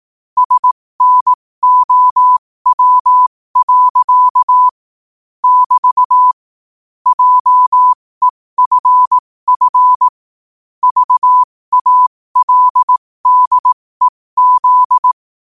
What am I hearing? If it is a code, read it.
Morse code, heard as a keyed tone, about 18 wpm, spaced as for 15 wpm, 1000 Hz. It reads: SNOW. <BT> JEFF VALDEZ
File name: qotdfiles / 15wpm